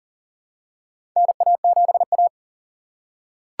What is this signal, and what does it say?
Morse code, heard as a keyed tone, about 40 wpm, 700 Hz. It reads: NA7A
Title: USA Call Signs - 4 Characters - 4x Character Spacing - 40wpm